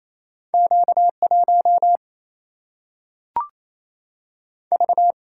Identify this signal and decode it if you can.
Morse code, heard as a keyed tone, about 28 wpm, 700 Hz. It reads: Q 1 V
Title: Single-Letter Single-Number 28wpm